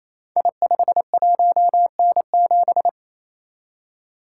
Morse code, heard as a keyed tone, about 29 wpm, 700 Hz. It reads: I51N7
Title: Code Groups - 5 Letter:Numbers - 28wpm